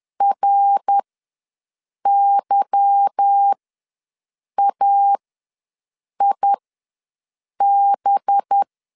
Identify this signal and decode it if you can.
Morse code, heard as a keyed tone, about 10 wpm, spaced as for 6 wpm, 800 Hz. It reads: RYAIB
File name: LEZIONI / 8 YJ45 / 8 Y ws 10wpm 3952